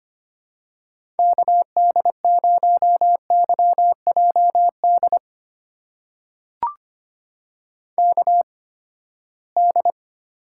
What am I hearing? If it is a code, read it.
Morse code, heard as a keyed tone, about 25 wpm, spaced as for 17 wpm, 700 Hz. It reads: KD0YJD KD
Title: USA Call Signs - 6 Characters - 8x Character Spacing - 25wpm